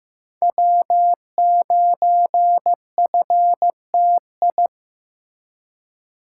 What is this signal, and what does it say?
Morse code, heard as a keyed tone, about 15 wpm, 700 Hz. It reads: W9FTI
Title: USA Call Signs - 5 Characters - 2x Character Spacing - 15wpm